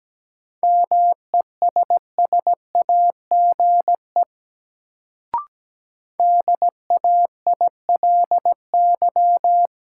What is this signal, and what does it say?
Morse code, heard as a keyed tone, about 17 wpm, 700 Hz. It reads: MESSAGE DAILY
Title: Sentences from Top 1100 Words - New Word Drill - 17wpm